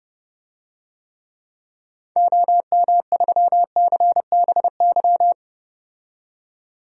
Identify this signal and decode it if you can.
Morse code, heard as a keyed tone, about 32 wpm, 700 Hz. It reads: OM3CBY
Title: DXCC-100 Call Signs - 6 Characters - 30wpm